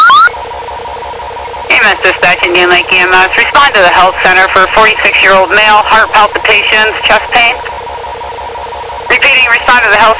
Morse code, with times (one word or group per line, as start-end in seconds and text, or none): none